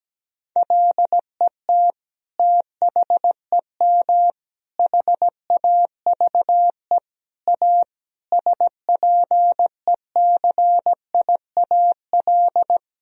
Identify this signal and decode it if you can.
Morse code, heard as a keyed tone, about 17 wpm, 700 Hz. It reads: LET THEM HAVE A SPECIAL
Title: Sentences from Top 1000 Words - Review Part 3 - 17wpm